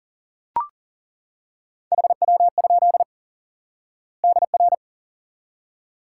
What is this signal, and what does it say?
Morse code, heard as a keyed tone, about 37 wpm, 700 Hz. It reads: HW? DR
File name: IQE Course 3 - Lesson 89 - YAGI - Two Elements - 40wpm